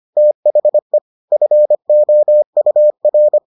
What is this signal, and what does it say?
Morse code, heard as a keyed tone, about 25 wpm, 600 Hz. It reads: THE FOUR